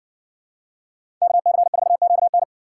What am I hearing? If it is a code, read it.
Morse code, heard as a keyed tone, about 57 wpm, 700 Hz. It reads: KC4QN